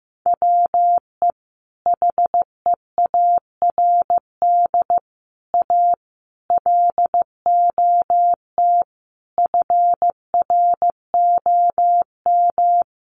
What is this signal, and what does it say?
Morse code, heard as a keyed tone, about 15 wpm, 700 Hz. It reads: WE HEARD A LOT FROM
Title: Sentences from Top 1100 Words 15wpm